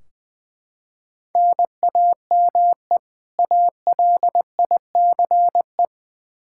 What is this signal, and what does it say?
Morse code, heard as a keyed tone, about 20 wpm, 700 Hz. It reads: NAME ALICE
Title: Practice QSO - Exchange 1 - Beginning - 20wpm